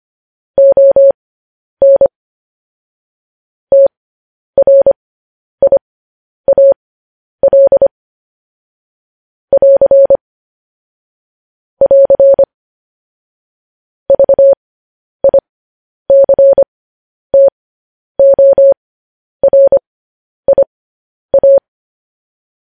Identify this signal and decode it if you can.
Morse code, heard as a keyed tone, about 25 wpm, spaced as for 10 wpm, 550 Hz. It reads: ON TRIAL <AR> <AR> VICTORIA